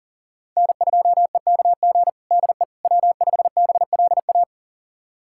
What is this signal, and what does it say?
Morse code, heard as a keyed tone, about 40 wpm, 700 Hz. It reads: N2EKG DE W5BLA
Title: USA Callsign DE Callsign - 5 Characters - 40wpm